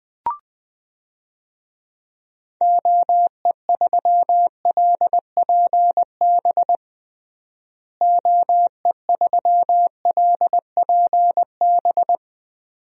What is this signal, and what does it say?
Morse code, heard as a keyed tone, about 20 wpm, 700 Hz. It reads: OE3LPB OE3LPB
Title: DXCC-100 Call Signs - 6 Characters - Repeated 2x - 20wpm